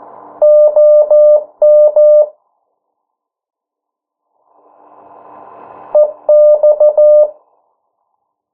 Morse code, heard as a keyed tone, about 14 wpm, 600 Hz. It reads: OM EX
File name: morse mp3 / 09 urok 8